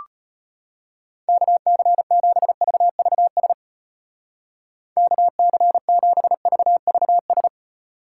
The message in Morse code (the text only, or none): KC7VVS KC7VVS